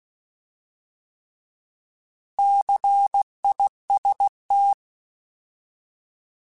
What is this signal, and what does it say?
Morse code, heard as a keyed tone, about 16 wpm, 800 Hz. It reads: CIST
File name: Alphabetical / Alphabetical mp3s / Cs @15